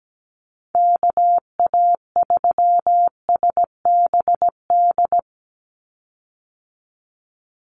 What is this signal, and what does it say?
Morse code, heard as a keyed tone, about 17 wpm, 700 Hz. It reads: KA3SBD
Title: USA Call Signs 17wpm